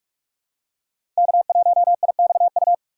Morse code, heard as a keyed tone, about 47 wpm, 700 Hz. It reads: K1IXU
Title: USA Call Signs - 5 Characters - 6x Character Spacing - 45wpm